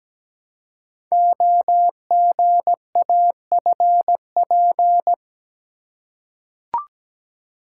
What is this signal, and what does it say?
Morse code, heard as a keyed tone, about 17 wpm, 700 Hz. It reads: OGAFP
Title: Code Groups - 5 Letters - 17wpm